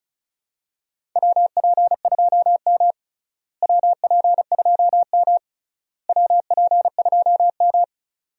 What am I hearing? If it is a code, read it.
Morse code, heard as a keyed tone, about 36 wpm, 700 Hz. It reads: WP2M WP2M WP2M
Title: USA Call Signs - 4 Characters - Repeated 3x - 35wpm